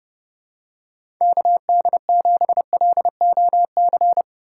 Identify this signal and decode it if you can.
Morse code, heard as a keyed tone, about 30 wpm, 700 Hz. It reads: KD7LOC
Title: USA Call Signs - 6 Characters - 3x Character Spacing - 30wpm